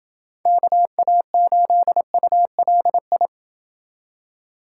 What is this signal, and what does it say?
Morse code, heard as a keyed tone, about 28 wpm, 700 Hz. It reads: KA8ULI